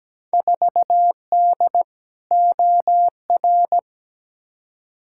Morse code, heard as a keyed tone, about 17 wpm, 700 Hz. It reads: 4D OR
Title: Field Day Exchanges Encore - 17wpm